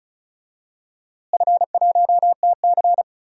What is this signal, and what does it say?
Morse code, heard as a keyed tone, about 34 wpm, 700 Hz. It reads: F1TC
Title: DXCC-100 Call Signs - 4 Characters - Repeated 2x - 35wpm